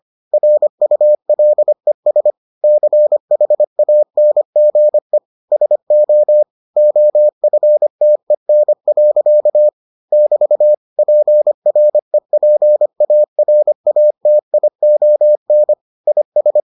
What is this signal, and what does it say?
Morse code, heard as a keyed tone, about 25 wpm, 600 Hz. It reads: RULES CHANGE SO OFTEN. <BT> PREPARATION IS